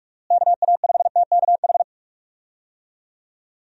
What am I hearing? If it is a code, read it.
Morse code, heard as a keyed tone, about 46 wpm, 700 Hz. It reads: KA5TKH
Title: USA Call Signs - 6 Characters - Repeated 2x - 45wpm